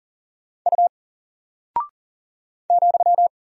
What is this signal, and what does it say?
Morse code, heard as a keyed tone, about 39 wpm, 700 Hz. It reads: U ,
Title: Single Character (Letter:Number:Punctuation:Prosign) 40wpm